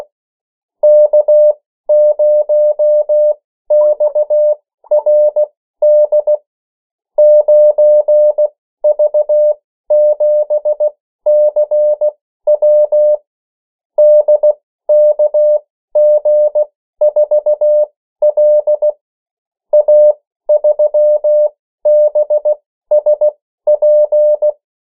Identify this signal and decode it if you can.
Morse code, heard as a keyed tone, about 16 wpm, spaced as for 13 wpm, 600 Hz. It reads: K0XRD 9V7CW DKG4L A3BSP
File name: morse mp3 / 14 urok 13